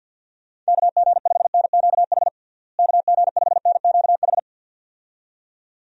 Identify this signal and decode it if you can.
Morse code, heard as a keyed tone, about 51 wpm, 700 Hz. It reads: KG5NQH KG5NQH
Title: USA Call Signs - 6 Characters - Repeated 2x - 50wpm